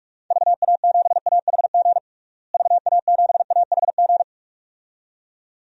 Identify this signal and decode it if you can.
Morse code, heard as a keyed tone, about 46 wpm, 700 Hz. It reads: VA7AHG VA7AHG
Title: DXCC-100 Call Signs - 6 Characters - Repeated 2x - 45wpm